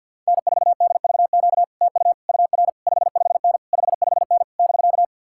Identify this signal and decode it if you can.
Morse code, heard as a keyed tone, about 66 wpm, spaced as for 57 wpm, 700 Hz. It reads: N4DVQ TU UR 55N 55N <BK>
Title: Speed-Racing Common QSO Phrases - 50wpm